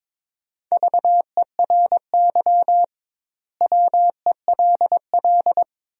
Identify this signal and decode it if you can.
Morse code, heard as a keyed tone, about 22 wpm, 700 Hz. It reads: VERY WELL
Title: 2x Word-Spacing - Sentences from Top 200 Words - 22wpm